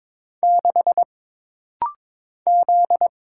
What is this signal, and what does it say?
Morse code, heard as a keyed tone, about 22 wpm, 700 Hz. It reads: 6 Z
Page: This rapid-fire timing is geared to push your ICR (Instant Character Recognition) skills.